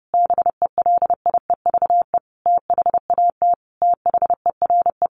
Morse code, heard as a keyed tone, about 30 wpm, 700 Hz. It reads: BELIEVE THAT THERE
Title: Sentences from Top 1000 Words 30wpm